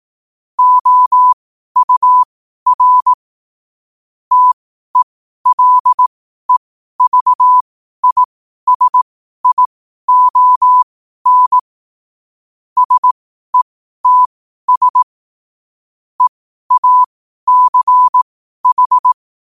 Morse code, heard as a keyed tone, about 18 wpm, spaced as for 12 wpm, 1000 Hz. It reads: OUR TELEVISION SETS EACH
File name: qotdfiles / 12wpm